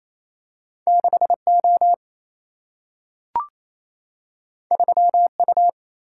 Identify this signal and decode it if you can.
Morse code, heard as a keyed tone, about 28 wpm, 700 Hz. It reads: 6O 3U
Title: Single-Letter Single-Number 28wpm